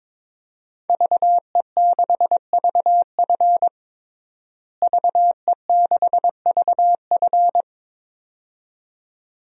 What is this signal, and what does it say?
Morse code, heard as a keyed tone, about 22 wpm, 700 Hz. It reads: VE6VF VE6VF
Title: DXCC-100 Call Signs - 5 Characters - Repeated 2x - 22wpm